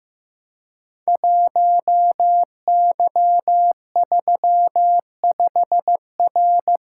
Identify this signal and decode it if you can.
Morse code, heard as a keyed tone, about 15 wpm, 700 Hz. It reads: 1Y35R
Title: Code Groups - 5 Letter:Numbers - 15wpm